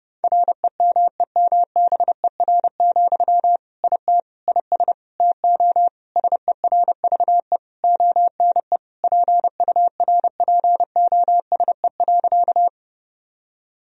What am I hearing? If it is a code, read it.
Morse code, heard as a keyed tone, about 30 wpm, 700 Hz. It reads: REMEMBER, IT IS TO SERVE ONE PURPOSE.
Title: Sentences from Top 1000 Words - Review Part 2 - 30wpm